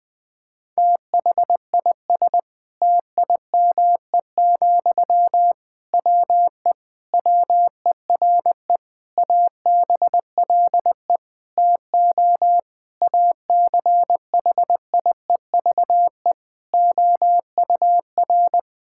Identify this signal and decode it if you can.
Morse code, heard as a keyed tone, about 20 wpm, 700 Hz. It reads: THIS TIME, WE WERE ABLE TO ACHIEVE OUR
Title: Sentences from Top 1000 Words - Review Part 6 - 20wpm